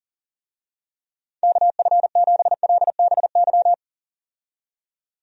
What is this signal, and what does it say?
Morse code, heard as a keyed tone, about 40 wpm, 700 Hz. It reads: KF7LBY
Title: USA Call Signs - 6 Characters - 40wpm